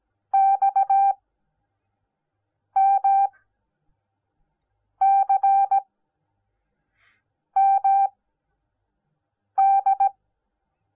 Morse code, heard as a keyed tone, about 17 wpm, spaced as for 5 wpm, 780 Hz. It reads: XMCMD